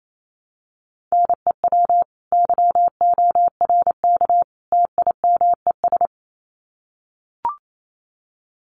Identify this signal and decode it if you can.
Morse code, heard as a keyed tone, about 28 wpm, 700 Hz. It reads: NEW YORK TIMES
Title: Sets of 3 Words 28wpm